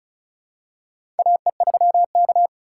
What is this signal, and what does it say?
Morse code, heard as a keyed tone, about 36 wpm, 700 Hz. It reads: AE3K
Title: USA Call Signs - 4 Characters - 4x Character Spacing - 35wpm